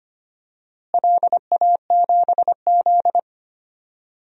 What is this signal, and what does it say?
Morse code, heard as a keyed tone, about 25 wpm, 700 Hz. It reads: LA7Z
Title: DXCC-100 Call Signs - 4 Characters - Repeated 3x - 25wpm